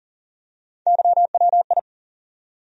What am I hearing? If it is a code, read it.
Morse code, heard as a keyed tone, about 40 wpm, 700 Hz. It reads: YWI
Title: USA Call Sign Postfixes - 3 Letters 40wpm